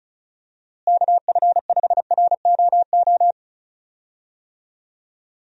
Morse code, heard as a keyed tone, about 34 wpm, 700 Hz. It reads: KF5ROO
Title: USA Call Signs - 6 Characters - Repeated 3x - 35wpm